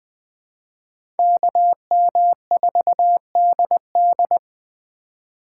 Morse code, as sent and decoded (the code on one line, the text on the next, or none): -.- -- ....- -.. -..
KM4DD